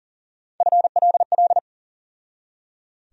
Morse code, heard as a keyed tone, about 39 wpm, 700 Hz. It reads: FLL
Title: USA Call Sign Postfixes - 3 Letters 40wpm